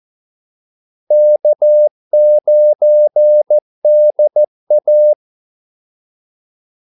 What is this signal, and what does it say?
Morse code, heard as a keyed tone, about 14 wpm, 600 Hz. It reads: K9DA
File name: Callsigns-14wpm